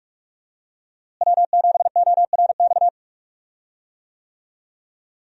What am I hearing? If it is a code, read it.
Morse code, heard as a keyed tone, about 46 wpm, 700 Hz. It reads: W7ORX
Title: USA Call Signs - 5 Characters - Repeated 2x - 45wpm